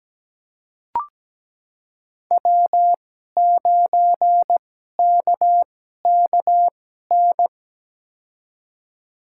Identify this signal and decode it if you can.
Morse code, heard as a keyed tone, about 17 wpm, spaced as for 12 wpm, 700 Hz. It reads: W9KKN
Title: USA Call Signs - 5 Characters - 2x Character Spacing - 17wpm